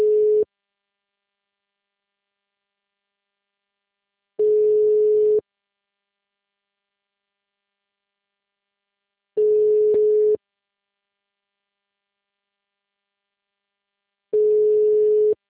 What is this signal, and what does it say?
Ringback tone (Europe, 425 Hz).